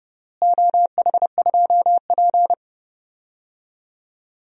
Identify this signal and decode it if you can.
Morse code, heard as a keyed tone, about 30 wpm, 700 Hz. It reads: OH2P
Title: DXCC-100 Call Signs - 4 Characters - Repeated 3x - 30wpm